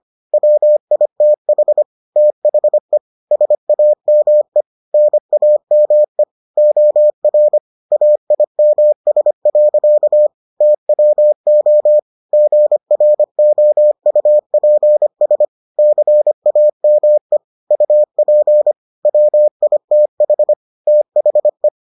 Morse code, heard as a keyed tone, about 25 wpm, 600 Hz. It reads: WITH THE SAME NAME OR AIMS. TWO GROUPS CAME UP WITH THE